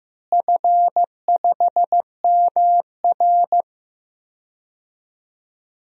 Morse code, heard as a keyed tone, about 15 wpm, 700 Hz. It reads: F5MR
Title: DXCC-100 Call Signs - 4 Characters - Repeated 3x - 15wpm